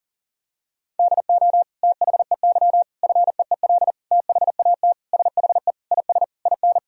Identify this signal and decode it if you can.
Morse code, heard as a keyed tone, about 40 wpm, 700 Hz. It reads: DO THEY FEEL THAT SHE IS IN
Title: Sentences from Top 1000 Words - Review Part 1 - 40wpm